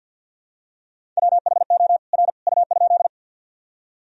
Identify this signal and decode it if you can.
Morse code, heard as a keyed tone, about 53 wpm, 700 Hz. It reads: WHO R U?
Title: Sentences from Top 100 Words and Abbreviations 50wpm